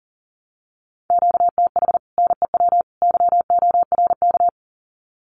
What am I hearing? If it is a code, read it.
Morse code, heard as a keyed tone, about 37 wpm, 700 Hz. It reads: QTH NEW YORK
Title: Common QSO Phrases 40wpm